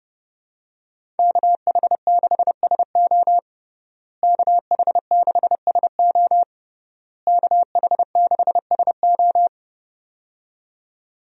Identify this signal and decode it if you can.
Morse code, heard as a keyed tone, about 30 wpm, 700 Hz. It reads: KH6SO KH6SO KH6SO
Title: USA Call Signs - 5 Characters - Repeated 3x - 30wpm